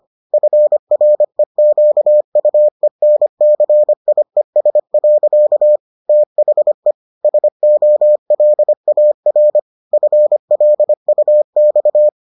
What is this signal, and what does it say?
Morse code, heard as a keyed tone, about 25 wpm, 600 Hz. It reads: FREQUENCIES. THE SOLAR FLUX